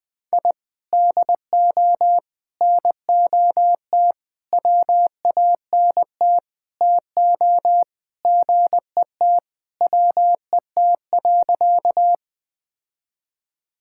Morse code, heard as a keyed tone, about 20 wpm, 700 Hz. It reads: I DO NOT WANT TO GET WET.
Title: Sentences with 5-Word Phrases - Spoken Beginning - 20wpm